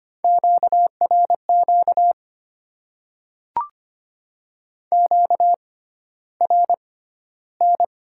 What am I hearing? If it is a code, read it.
Morse code, heard as a keyed tone, about 25 wpm, spaced as for 13 wpm, 700 Hz. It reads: QRQ QRN
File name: Top 90 QSO Elements - 6x Character Spacing - 25wpm